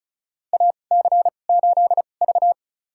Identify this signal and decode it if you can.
Morse code, heard as a keyed tone, about 36 wpm, spaced as for 25 wpm, 700 Hz. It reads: AC8V
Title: USA Call Signs - 2x Character Spacing - 35wpm